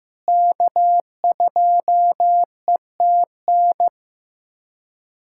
Morse code, heard as a keyed tone, about 16 wpm, 700 Hz. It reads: K2ETN
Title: USA Callsign DE Callsign - 5 Characters - 15wpm